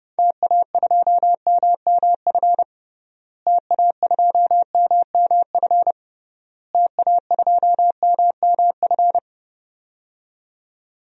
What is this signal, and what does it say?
Morse code, heard as a keyed tone, about 30 wpm, 700 Hz. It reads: TA2MMF TA2MMF TA2MMF
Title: DXCC-100 Call Signs - 6 Characters - Repeated 3x - 30wpm